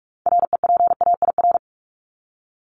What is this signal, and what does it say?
Morse code, heard as a keyed tone, about 44 wpm, 700 Hz. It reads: REPAIR
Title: Top 1501-1750 Words 45wpm